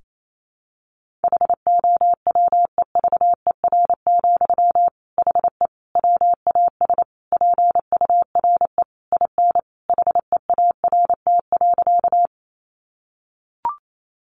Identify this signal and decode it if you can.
Morse code, heard as a keyed tone, about 28 wpm, 700 Hz. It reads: HOWEVER, HE WAS PURE IN HEART.
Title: Sentences from Top 1500 Words 28wpm